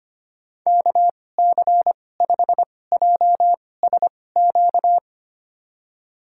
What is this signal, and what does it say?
Morse code, heard as a keyed tone, about 25 wpm, spaced as for 18 wpm, 700 Hz. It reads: KC5JSQ
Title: USA Call Signs - 6 Characters - 2x Character Spacing - 25wpm